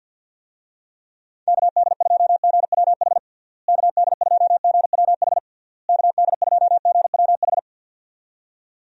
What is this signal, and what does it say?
Morse code, heard as a keyed tone, about 51 wpm, 700 Hz. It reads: KD2GWH KD2GWH KD2GWH
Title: USA Call Signs - 6 Characters - Repeated 3x - 50wpm